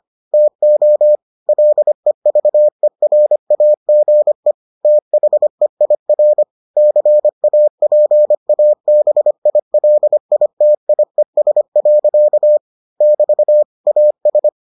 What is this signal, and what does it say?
Morse code, heard as a keyed tone, about 25 wpm, 600 Hz. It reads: TO LEVERAGE THEIR CAPABILITIES. <BT> AS